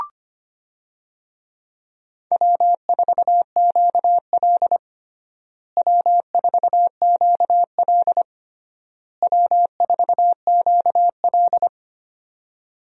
Morse code, heard as a keyed tone, about 25 wpm, 700 Hz. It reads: W4QL W4QL W4QL
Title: USA Call Signs - 4 Characters - Repeated 3x - 25wpm